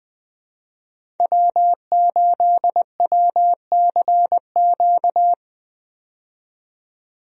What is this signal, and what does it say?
Morse code, heard as a keyed tone, about 20 wpm, 700 Hz. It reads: W8WCQ
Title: USA Call Signs - 5 Characters - 20wpm